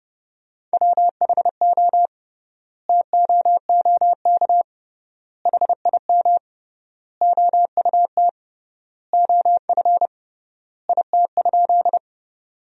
Morse code, heard as a keyed tone, about 30 wpm, 700 Hz. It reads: WHO TOOK HIM OUT OF IT?
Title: 3x Word-Spacing - Sentences from Top 300 Words - 30wpm